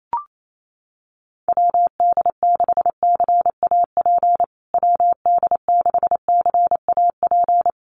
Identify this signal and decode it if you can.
Morse code, heard as a keyed tone, about 28 wpm, 700 Hz. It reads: WD6CAP WD6CAP
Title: USA Call Signs Repeated 2x 28wpm